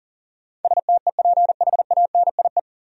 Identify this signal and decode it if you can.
Morse code, heard as a keyed tone, about 40 wpm, 700 Hz. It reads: STEPHANIE
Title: USA Common Names - 40wpm